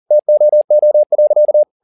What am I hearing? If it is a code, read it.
Morse code, heard as a keyed tone, about 40 wpm, 600 Hz. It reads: TOO.